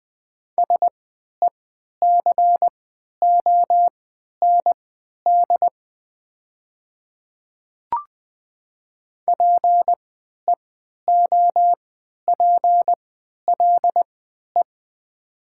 Morse code, heard as a keyed tone, about 20 wpm, spaced as for 11 wpm, 700 Hz. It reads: SECOND PEOPLE
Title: 3x Character Spacing - Top 300 Words - 20wpm